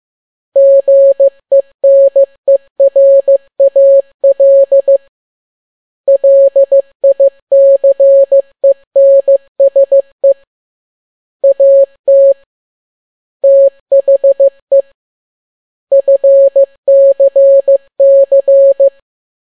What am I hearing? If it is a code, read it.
Morse code, heard as a keyed tone, about 15 wpm, 550 Hz. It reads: GENERAL LICENSE AT THE FCC
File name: n6tt p1 15C13